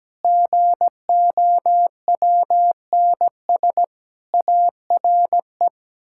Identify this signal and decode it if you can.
Morse code, heard as a keyed tone, about 17 wpm, 700 Hz. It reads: GOWNS ARE
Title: Pangram Challenge v2 - 17wpm